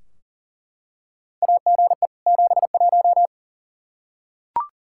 Morse code, heard as a keyed tone, about 39 wpm, 690 Hz. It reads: AGE 71
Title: Practice QSO - Exchange 3 - More Info - 40wpm